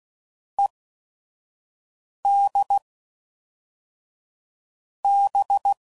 Morse code, heard as a keyed tone, about 16 wpm, spaced as for 5 wpm, 800 Hz. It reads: ED B
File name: CW Code Groups @5WPM / Technical Subjects / XC